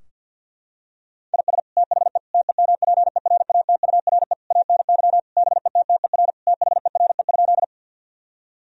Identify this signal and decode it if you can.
Morse code, heard as a keyed tone, about 49 wpm, 700 Hz. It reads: IS THE TEMPERATURE ANY BETTER THERE?
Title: Sentences from Top 1000 Words - Review Part 4 - 50wpm